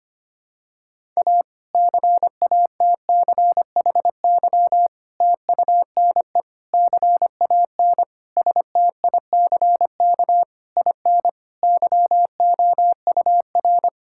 Morse code, heard as a keyed tone, about 25 wpm, 700 Hz. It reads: A CATCHY TUNE CAN STICK IN YOUR
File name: practice / 25wpm / Difficult Sentences with Hints - 25wpm